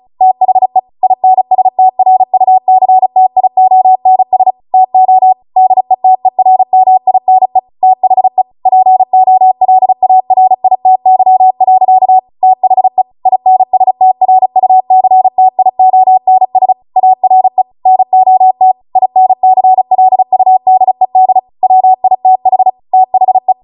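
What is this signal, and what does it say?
Morse code, heard as a keyed tone, about 36 wpm, 750 Hz. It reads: THE INSTRUCTIONS TO DETERMINE THE POLARITY. THE INSTRUCTIONS ARE NOT INCLUDED WITH THE
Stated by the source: Amateur radio Morse code practice archive transmission at 35 wpm.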